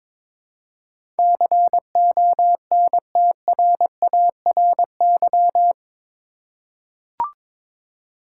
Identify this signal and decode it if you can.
Morse code, heard as a keyed tone, about 22 wpm, 700 Hz. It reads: CONTRARY
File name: practice / 22wpm / Eight Letter Words - 22wpm